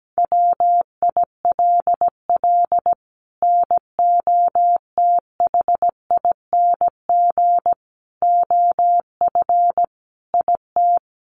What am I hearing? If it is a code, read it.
Morse code, heard as a keyed tone, about 17 wpm, 700 Hz. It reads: WILL NOTHING OF IT
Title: Sentences from Top 1200 Words 17wpm